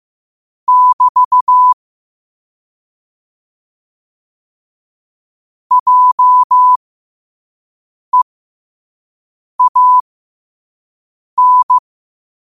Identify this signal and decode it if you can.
Morse code, heard as a keyed tone, about 15 wpm, spaced as for 5 wpm, 1000 Hz. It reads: <BT> JEAN
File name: qotdfiles / 5wpm